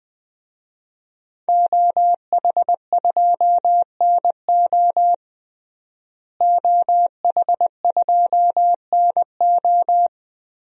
Morse code, heard as a keyed tone, about 20 wpm, 700 Hz. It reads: OH2NO OH2NO
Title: DXCC-100 Call Signs - 5 Characters - Repeated 2x - 20wpm